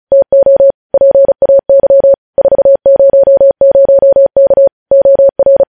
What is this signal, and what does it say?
Morse code, heard as a keyed tone, about 36 wpm, 550 Hz. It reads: TO PAY 400K OR
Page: news headlines in morse code at 35 wpm